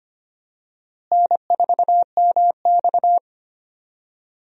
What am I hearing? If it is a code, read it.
Morse code, heard as a keyed tone, about 25 wpm, 700 Hz. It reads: N4MX
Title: USA Call Signs - 4 Characters - Repeated 3x - 25wpm